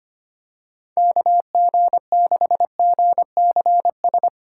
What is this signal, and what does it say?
Morse code, heard as a keyed tone, about 25 wpm, 700 Hz. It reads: KG6GCS
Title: USA Call Signs - 4x Character Spacing - 25wpm